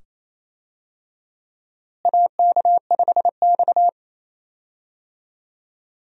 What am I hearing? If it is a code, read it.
Morse code, heard as a keyed tone, about 27 wpm, 700 Hz. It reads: AK5X
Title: USA Call Signs - 4 Characters - Repeated 2x - 28wpm